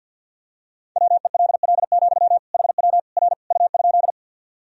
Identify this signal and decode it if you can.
Morse code, heard as a keyed tone, about 52 wpm, 700 Hz. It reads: WELL, HW R U?
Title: Sentences from Top 200 Words and Abbreviations 50wpm